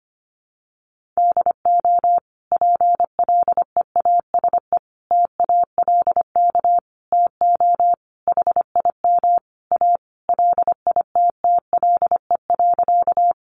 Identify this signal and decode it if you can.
Morse code, heard as a keyed tone, about 25 wpm, 700 Hz. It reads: DO PLEASE TALK TO HIM A LITTLE.
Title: Sentences from Top 1500 Words 25wpm